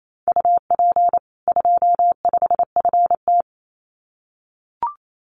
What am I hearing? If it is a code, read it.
Morse code, heard as a keyed tone, about 28 wpm, 700 Hz. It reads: UP 25FT
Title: Common QSO Phrases 28wpm